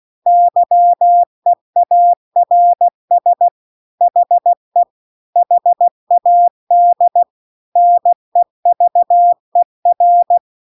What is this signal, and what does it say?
Morse code, heard as a keyed tone, about 16 wpm, 700 Hz. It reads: YEARS HE HAD NEVER